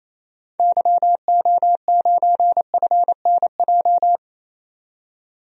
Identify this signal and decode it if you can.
Morse code, heard as a keyed tone, about 27 wpm, 700 Hz. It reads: YO9FNJ
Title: DXCC-100 Call Signs - 6 Characters - 28wpm